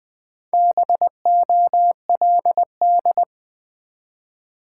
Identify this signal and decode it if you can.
Morse code, heard as a keyed tone, about 20 wpm, 700 Hz. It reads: BOLD
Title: Build A Word - 2 Letters At A Time - 4-Letter Words - 20wpm